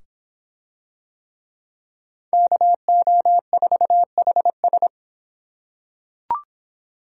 Morse code, heard as a keyed tone, about 24 wpm, 700 Hz. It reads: KO4HS